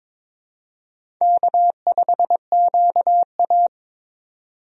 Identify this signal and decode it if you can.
Morse code, heard as a keyed tone, about 22 wpm, 700 Hz. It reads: K5QA
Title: USA Call Signs - 4 Characters - 22wpm